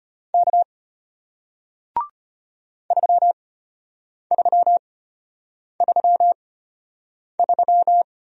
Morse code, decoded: K 3333

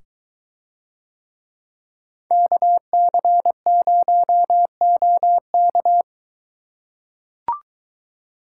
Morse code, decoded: KC0OK

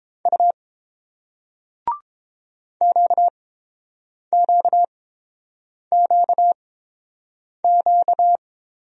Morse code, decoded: U QQQQ

